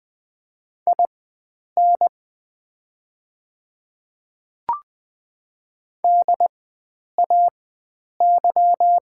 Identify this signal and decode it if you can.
Morse code, heard as a keyed tone, about 20 wpm, spaced as for 9 wpm, 690 Hz. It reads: IN DAY
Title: 4x Character Spacing - Top 100 Words - 20wpm